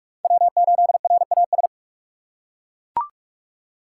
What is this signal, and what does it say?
Morse code, heard as a keyed tone, about 41 wpm, 700 Hz. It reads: W8RAS
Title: USA Call Signs - 5 Characters - 2x Character Spacing - 45wpm